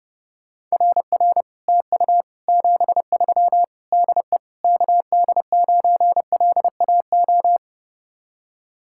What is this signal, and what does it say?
Morse code, heard as a keyed tone, about 38 wpm, 700 Hz. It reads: RR TU 73 DE KD9LAO